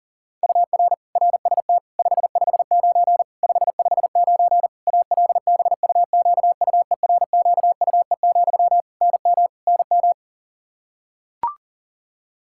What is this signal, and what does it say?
Morse code, heard as a keyed tone, about 40 wpm, 700 Hz. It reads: UR RST 559 559 ALBUQUERQUE, NM NM
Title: Practice QSO - Exchange 1 - Beginning - 40wpm